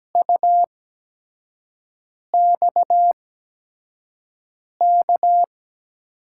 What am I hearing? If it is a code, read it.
Morse code, heard as a keyed tone, about 17 wpm, spaced as for 5 wpm, 700 Hz. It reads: UXK
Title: USA Call Signs - 8x Character Spacing - 17wpm